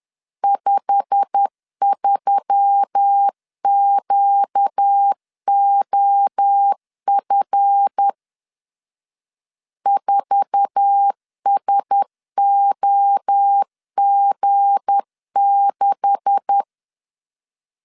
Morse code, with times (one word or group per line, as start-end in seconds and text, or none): 0.4-8.1 53QOF
9.9-16.6 4SOG6